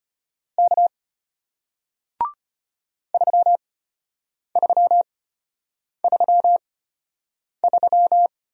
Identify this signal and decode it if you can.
Morse code, heard as a keyed tone, about 34 wpm, spaced as for 8 wpm, 690 Hz. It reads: K 3333